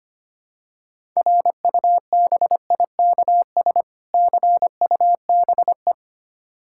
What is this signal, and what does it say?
Morse code, heard as a keyed tone, about 25 wpm, 700 Hz. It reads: RUBIKS CUBE
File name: practice / 25wpm / Difficult Words with Hints - 25wpm